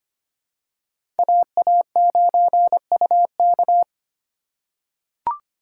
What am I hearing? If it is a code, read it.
Morse code, heard as a keyed tone, about 25 wpm, 700 Hz. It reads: AA9UK
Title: USA Call Signs - 5 Characters - 6x Character Spacing - 25wpm